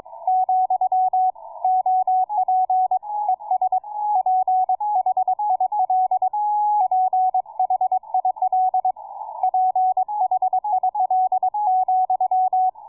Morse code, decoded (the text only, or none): , OP ES PHIL PHIL PHIL ,